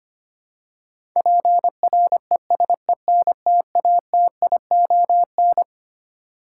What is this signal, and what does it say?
Morse code, heard as a keyed tone, about 25 wpm, 700 Hz. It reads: PRESENTATION
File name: Twelve Letter Words - 25wpm